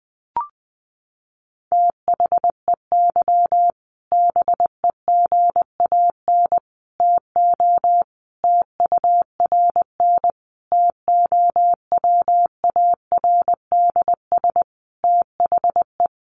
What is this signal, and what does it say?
Morse code, heard as a keyed tone, about 20 wpm, 700 Hz. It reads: THEY BEGAN TO TURN TOWARDS THE
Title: Sentences from Top 1100 Words 20wpm